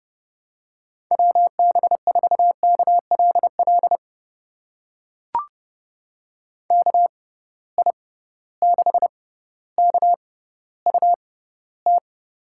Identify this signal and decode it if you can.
Morse code, heard as a keyed tone, about 30 wpm, spaced as for 15 wpm, 700 Hz. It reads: WB4KLL KI6KUT
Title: USA Call Signs - 6 Characters - 6x Character Spacing - 30wpm